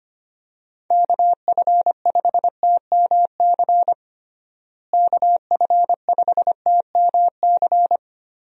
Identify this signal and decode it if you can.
Morse code, heard as a keyed tone, about 25 wpm, 700 Hz. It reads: KF5TMC KF5TMC
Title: USA Call Signs - 6 Characters - Repeated 3x - 25wpm